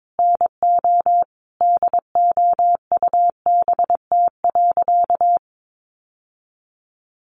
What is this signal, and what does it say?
Morse code, heard as a keyed tone, about 22 wpm, 700 Hz. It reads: NO DOUBT.
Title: Sentences from Top 1500 Words 22wpm